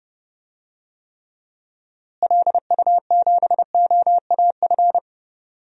Morse code, heard as a keyed tone, about 30 wpm, 700 Hz. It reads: LU7OAF
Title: DXCC-100 Call Signs - 6 Characters - Repeated 2x - 30wpm